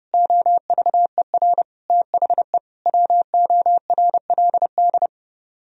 Morse code, heard as a keyed tone, about 31 wpm, 700 Hz. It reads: OVER THE WORLD